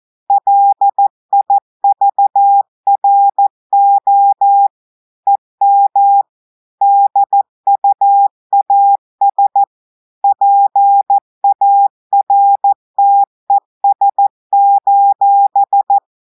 Morse code, heard as a keyed tone, about 14 wpm, 800 Hz. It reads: LIVRO EM DUAS PARTES:
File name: audios / portugais / portugais texte en clair 1 vitesse 14 mots minutes